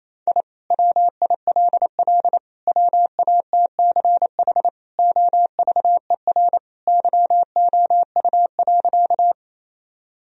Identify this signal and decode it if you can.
Morse code, heard as a keyed tone, about 28 wpm, 700 Hz. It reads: I WILL WATCH OVER YOU.
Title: Sentences from Top 1000 Words - Review Part 1 - 28wpm